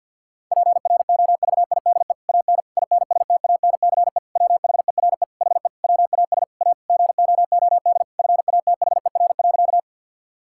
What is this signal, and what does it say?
Morse code, heard as a keyed tone, about 50 wpm, 700 Hz. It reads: PROVIDE AN INSTANCE WHERE HE WAS A GOOD FATHER.
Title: Sentences from Top 1000 Words - Review Part 5 - 50wpm